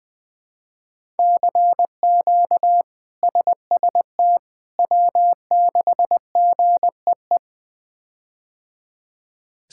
Morse code, heard as a keyed tone, about 20 wpm, 700 Hz. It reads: CQ SST W6GEE
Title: K1USN SST QSO Practice - 20wpm